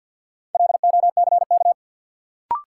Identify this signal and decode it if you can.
Morse code, heard as a keyed tone, about 53 wpm, 700 Hz. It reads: LOCK
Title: Sentences from Top 1300 Words - New Word Drill - 50wpm